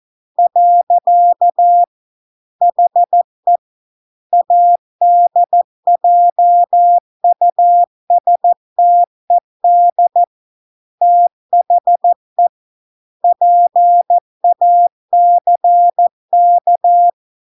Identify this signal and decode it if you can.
Morse code, heard as a keyed tone, about 14 wpm, 700 Hz. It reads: . HE ADJUSTED THE PACK